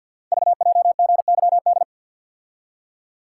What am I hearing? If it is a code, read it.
Morse code, heard as a keyed tone, about 46 wpm, 700 Hz. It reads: VJGYD
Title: Code Groups - 5 Letters - 50wpm